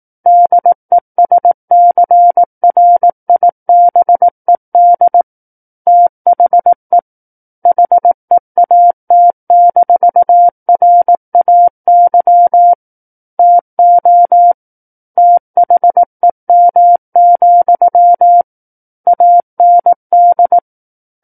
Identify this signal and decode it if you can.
Morse code, heard as a keyed tone, about 18 wpm, 700 Hz. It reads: DESCRIBED THE HEAT-RAY TO THEM, AND
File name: Book One / 09-Chapter 9 - 18 WPM